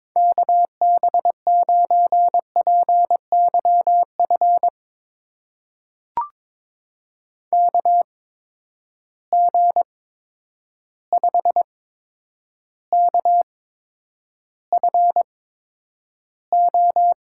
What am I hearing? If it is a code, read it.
Morse code, heard as a keyed tone, about 23 wpm, spaced as for 10 wpm, 700 Hz. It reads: KB9PYF KG5KFO
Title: USA Call Signs - 6 Characters - 8x Character Spacing - 22wpm